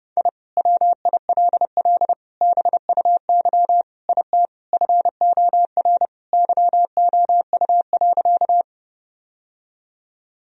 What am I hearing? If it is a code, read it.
Morse code, heard as a keyed tone, about 30 wpm, 700 Hz. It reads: I WILL BUY IT FOR YOU.
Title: Sentences from Top 1000 Words - Review Part 4 - 30wpm